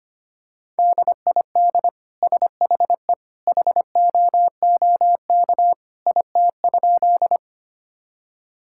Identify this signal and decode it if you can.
Morse code, heard as a keyed tone, about 25 wpm, 700 Hz. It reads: DID SHE HOOK IT?
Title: Sentences from Top 1600 Words 25wpm